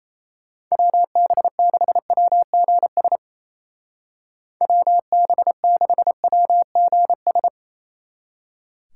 Morse code, heard as a keyed tone, about 35 wpm, 700 Hz. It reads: WB6WGS WB6WGS
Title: Speed Racing - USA 2x3 Call Signs 28wpm